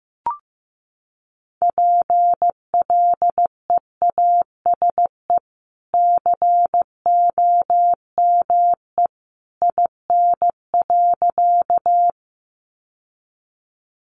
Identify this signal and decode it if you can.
Morse code, heard as a keyed tone, about 15 wpm, 700 Hz. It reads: PLEASE COME IN.
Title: Sentences from Top 1500 Words 15wpm